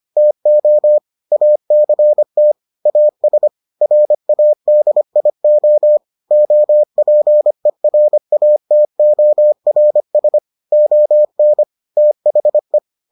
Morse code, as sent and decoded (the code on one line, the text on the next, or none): - --- / .- -.-. - / .- ... / .-. .- -.. .. --- / --- .--. .-. .- - --- .-. ... / --- -. / - .... .
TO ACT AS RADIO OPERATORS ON THE